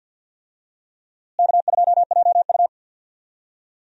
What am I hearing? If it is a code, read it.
Morse code, heard as a keyed tone, about 50 wpm, 700 Hz. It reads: K2JU
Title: USA Call Signs - 4 Characters - Repeated 2x - 50wpm